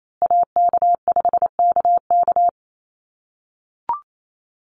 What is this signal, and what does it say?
Morse code, heard as a keyed tone, about 28 wpm, 700 Hz. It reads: AK5KK
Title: USA Call Signs Repeated 2x 28wpm